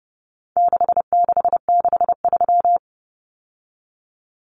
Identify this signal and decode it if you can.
Morse code, heard as a keyed tone, about 30 wpm, 700 Hz. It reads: 6663
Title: Numbers - 4-Digit 30wpm